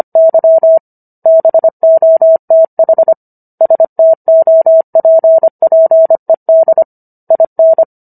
Morse code, heard as a keyed tone, about 25 wpm, 650 Hz. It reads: Y BOTH STOPPED IN
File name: Book One / 16-Chapter 16 - 25 WPM